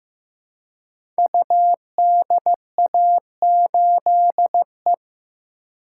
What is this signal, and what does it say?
Morse code, heard as a keyed tone, about 15 wpm, 700 Hz. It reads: UDA8E
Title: Code Groups - 5 Letter:Numbers - 15wpm